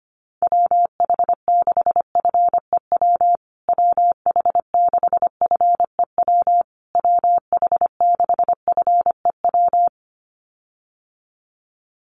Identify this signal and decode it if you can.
Morse code, heard as a keyed tone, about 25 wpm, 700 Hz. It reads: WH6FEW WH6FEW WH6FEW
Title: USA Call Signs Repeated 3x 25wpm